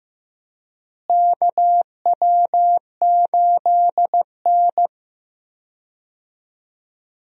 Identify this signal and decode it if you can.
Morse code, heard as a keyed tone, about 15 wpm, 700 Hz. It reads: KW8N
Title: USA Call Signs - 4 Characters - Repeated 2x - 15wpm